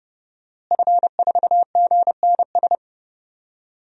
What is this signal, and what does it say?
Morse code, heard as a keyed tone, about 29 wpm, 700 Hz. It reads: F4GNS
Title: DXCC-100 Call Signs - 5 Characters - Repeated 3x - 30wpm